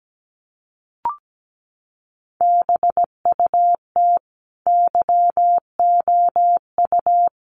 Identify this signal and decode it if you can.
Morse code, heard as a keyed tone, about 17 wpm, 700 Hz. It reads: BUT YOU
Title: Sentences from Top 1300 Words 17wpm